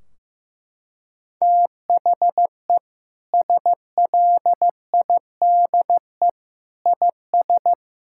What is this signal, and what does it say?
Morse code, heard as a keyed tone, about 15 wpm, 700 Hz. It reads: THE SLIDE IS
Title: Sentences from Top 1600 Words 15wpm